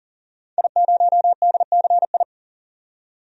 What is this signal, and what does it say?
Morse code, heard as a keyed tone, about 40 wpm, 700 Hz. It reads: I0DCI
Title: DXCC-100 Call Signs - 5 Characters - 40wpm